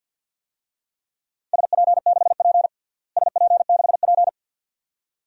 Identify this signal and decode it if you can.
Morse code, heard as a keyed tone, about 50 wpm, 700 Hz. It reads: SP6P SP6P
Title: DXCC-100 Call Signs - 4 Characters - Repeated 2x - 50wpm